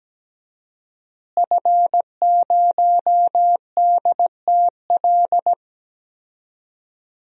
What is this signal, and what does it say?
Morse code, heard as a keyed tone, about 17 wpm, 700 Hz. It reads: F0DTL
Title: DXCC-100 Call Signs - 5 Characters - Repeated 3x - 17wpm